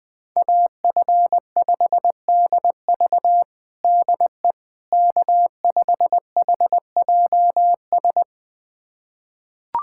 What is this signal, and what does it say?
Morse code, heard as a keyed tone, about 20 wpm, 700 Hz. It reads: AF5DV DE K5HJS
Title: USA Callsign DE Callsign - 5 Characters - 20wpm